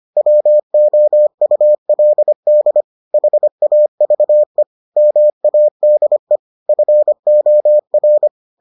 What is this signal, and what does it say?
Morse code, heard as a keyed tone, about 25 wpm, 600 Hz. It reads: WOULD HAVE MADE FOR